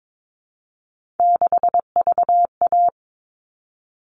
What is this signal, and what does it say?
Morse code, heard as a keyed tone, about 22 wpm, 700 Hz. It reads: 6VA
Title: All Cut Numbers - 3-Digit 22wpm